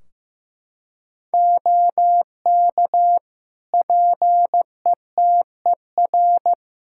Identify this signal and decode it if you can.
Morse code, heard as a keyed tone, about 15 wpm, 700 Hz. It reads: OK PETER
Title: K1USN SST QSO Practice 15wpm